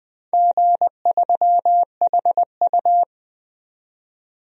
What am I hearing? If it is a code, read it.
Morse code, heard as a keyed tone, about 20 wpm, 700 Hz. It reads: G3HU
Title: DXCC-100 Call Signs - 4 Characters - 20wpm